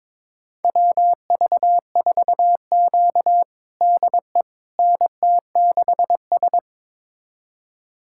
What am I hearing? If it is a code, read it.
Morse code, heard as a keyed tone, about 22 wpm, 700 Hz. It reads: WV4Q DE NT6S
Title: USA Callsign DE Callsign - 4 Characters - 22wpm